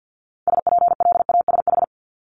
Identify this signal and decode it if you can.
Morse code, heard as a keyed tone, about 50 wpm, 700 Hz. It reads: SPLASH
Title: Multi-letter Phonemes Repeated 3x 50wpm